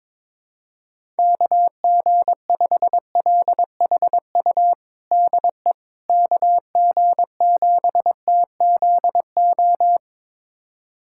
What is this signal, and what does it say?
Morse code, heard as a keyed tone, about 22 wpm, 700 Hz. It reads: KG5LHU DE KG7TZO
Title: USA Callsign DE Callsign - 6 Characters - 22wpm